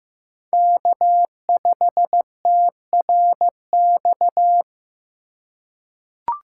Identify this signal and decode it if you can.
Morse code, heard as a keyed tone, about 15 wpm, 700 Hz. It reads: K5TRX
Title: USA Call Signs - 5 Characters - 2x Character Spacing - 15wpm